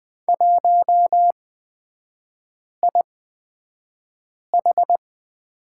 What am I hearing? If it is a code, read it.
Morse code, heard as a keyed tone, about 20 wpm, spaced as for 5 wpm, 700 Hz. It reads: 1IH
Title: 243 Lesson 39e - Letter z - Three Characters 20wpm